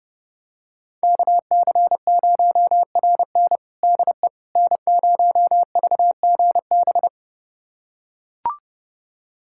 Morse code, heard as a keyed tone, about 30 wpm, 700 Hz. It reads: KC0RN DE N0VGB
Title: USA Callsign DE Callsign 30wpm